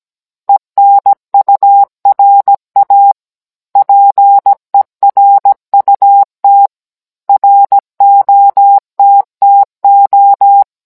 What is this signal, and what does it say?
Morse code, heard as a keyed tone, about 17 wpm, 800 Hz. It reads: ENURA PERUT ROTTO